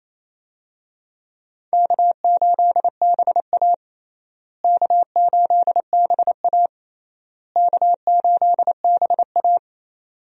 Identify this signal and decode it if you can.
Morse code, heard as a keyed tone, about 28 wpm, 700 Hz. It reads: K8BA K8BA K8BA
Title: USA Call Signs - 4 Characters - Repeated 3x - 28wpm